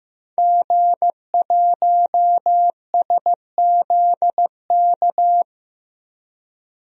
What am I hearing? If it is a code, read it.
Morse code, heard as a keyed tone, about 16 wpm, 700 Hz. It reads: G1SZK
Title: DXCC-100 Call Signs - 5 Characters - Repeated 2x - 15wpm